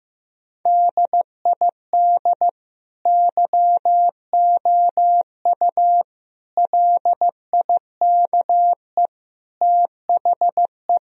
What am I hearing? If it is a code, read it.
Morse code, heard as a keyed tone, about 15 wpm, 700 Hz. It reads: DID YOU LIKE THE
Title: Sentences from Top 1000 Words - Review Part 4 - 15wpm